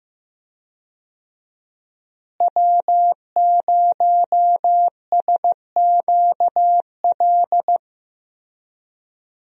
Morse code, heard as a keyed tone, about 15 wpm, 700 Hz. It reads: W0SQL
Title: USA Call Signs - 5 Characters - Repeated 2x - 15wpm